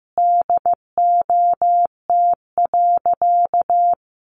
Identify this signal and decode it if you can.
Morse code, heard as a keyed tone, about 15 wpm, 700 Hz. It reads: DOT.
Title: Sentences from Top 1500 Words 15wpm